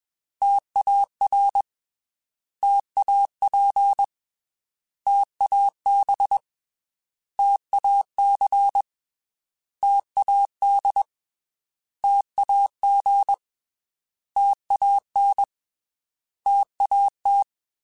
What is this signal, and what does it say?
Morse code, heard as a keyed tone, about 21 wpm, 800 Hz. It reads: TAR TAP TAB TAC TAD TAG TAN TAT
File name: T Prefixes / T mp3s